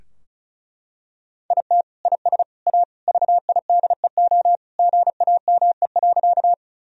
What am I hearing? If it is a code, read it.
Morse code, heard as a keyed tone, about 36 wpm, 700 Hz. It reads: IT IS A VIDEO GAME.
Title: Sentences from Top 1000 Words - Review Part 4 - 35wpm